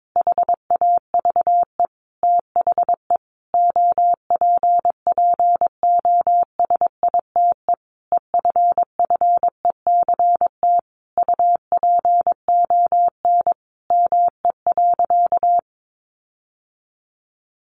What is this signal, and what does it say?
Morse code, heard as a keyed tone, about 22 wpm, 700 Hz. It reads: HAVE THE OPPOSITE EFFECT UPON ME.
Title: Sentences from Top 1300 Words 22wpm